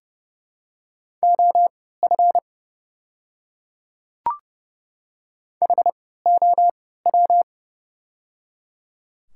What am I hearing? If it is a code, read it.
Morse code, heard as a keyed tone, about 30 wpm, spaced as for 17 wpm, 700 Hz. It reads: OF HOW
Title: 3x Character Spacing - Top 100 Words - 30wpm